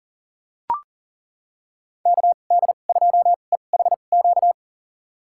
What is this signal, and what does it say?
Morse code, heard as a keyed tone, about 38 wpm, spaced as for 28 wpm, 700 Hz. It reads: KD2EHQ
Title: USA Call Signs - 2x Character Spacing - 40wpm